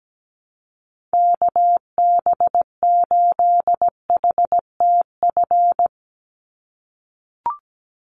Morse code, heard as a keyed tone, about 17 wpm, 700 Hz. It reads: KB8HTF
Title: USA Call Signs 17wpm